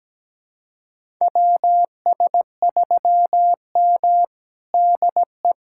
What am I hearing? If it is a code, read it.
Morse code, heard as a keyed tone, about 17 wpm, 700 Hz. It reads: WS3M DE
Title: USA Callsign DE Callsign - 4 Characters - 17wpm